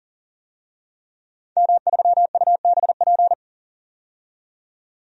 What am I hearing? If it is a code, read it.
Morse code, heard as a keyed tone, about 39 wpm, 700 Hz. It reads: M3UBP
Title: DXCC-100 Call Signs - 5 Characters - 40wpm